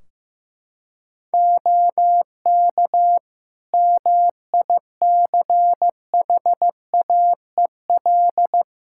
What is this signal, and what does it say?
Morse code, heard as a keyed tone, about 15 wpm, 700 Hz. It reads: OK MICHAEL
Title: Practice QSO - Exchange 4 - Ending - 15wpm